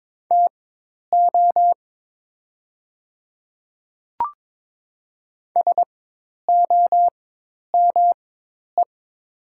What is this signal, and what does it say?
Morse code, heard as a keyed tone, about 22 wpm, spaced as for 10 wpm, 690 Hz. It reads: TO SOME
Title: 4x Character Spacing - Top 100 Words - 22wpm